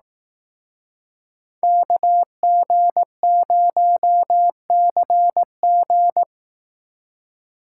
Morse code, heard as a keyed tone, about 18 wpm, 700 Hz. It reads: KG0CG
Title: Speed Racing - USA Call Signs 15wpm